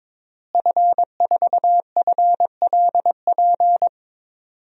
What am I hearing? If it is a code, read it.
Morse code, heard as a keyed tone, about 21 wpm, 700 Hz. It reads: F4FLP
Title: DXCC-100 Call Signs - 5 Characters - Repeated 3x - 22wpm